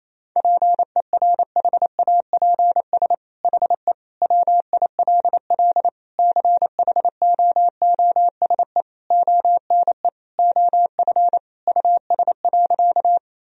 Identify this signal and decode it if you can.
Morse code, heard as a keyed tone, about 28 wpm, 700 Hz. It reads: PERHAPS HE WILL CHOOSE ONE OF US.
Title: Sentences from Top 1000 Words - Review Part 4 - 28wpm